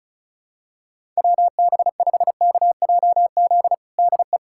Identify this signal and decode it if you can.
Morse code, heard as a keyed tone, about 34 wpm, 700 Hz. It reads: WB5KJZ DE
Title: USA Callsign DE Callsign - 6 Characters - 35wpm